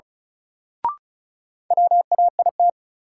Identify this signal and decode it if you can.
Morse code, heard as a keyed tone, about 36 wpm, 700 Hz. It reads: WAIT
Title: Sentences from Top 500 Words - New Word Drill - 35wpm